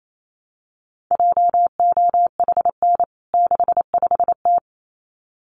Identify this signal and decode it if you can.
Morse code, heard as a keyed tone, about 28 wpm, 700 Hz. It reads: JOHN 65T
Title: CWT Exchange 28wpm